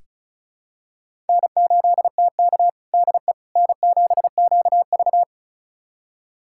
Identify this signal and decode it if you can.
Morse code, heard as a keyed tone, about 34 wpm, 700 Hz. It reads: N8TK DE N7QV
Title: USA Callsign DE Callsign - 4 Characters - 35wpm